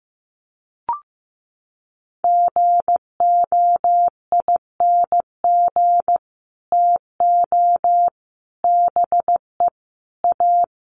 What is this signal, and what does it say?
Morse code, heard as a keyed tone, about 15 wpm, 700 Hz. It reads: GOING TO BE A
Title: Sets of 5 Words Encore 15wpm